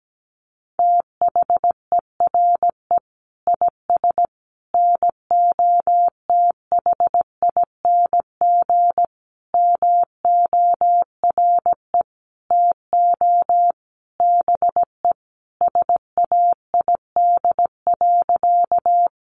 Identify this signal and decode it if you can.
Morse code, heard as a keyed tone, about 17 wpm, 700 Hz. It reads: THERE IS NOTHING MORE TO BE SAID.
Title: Sentences from Top 500 Words 17wpm